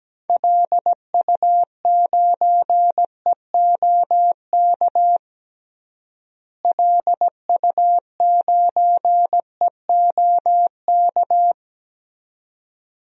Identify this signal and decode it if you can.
Morse code, heard as a keyed tone, about 17 wpm, 700 Hz. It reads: LU9EOK LU9EOK
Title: DXCC-100 Call Signs - 6 Characters - Repeated 2x - 17wpm